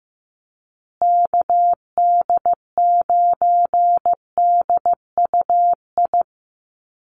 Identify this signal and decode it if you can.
Morse code, heard as a keyed tone, about 15 wpm, 700 Hz. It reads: KD9DUI
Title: USA Call Signs Repeated 3x 15wpm